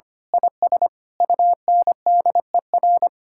Morse code, heard as a keyed tone, about 25 wpm, 700 Hz. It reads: IS UNDER
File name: practice / 25wpm / Sentences with 9-Word Phrases - 25wpm